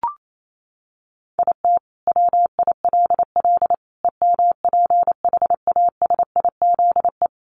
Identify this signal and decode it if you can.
Morse code, heard as a keyed tone, about 28 wpm, 700 Hz. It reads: IT WILL EMPHASIZE
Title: Sentences from Top 1500 Words 28wpm